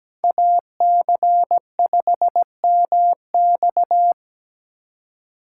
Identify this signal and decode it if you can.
Morse code, heard as a keyed tone, about 20 wpm, 700 Hz. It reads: AC5MX
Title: Speed Racing - USA 2x2 Call Signs 17wpm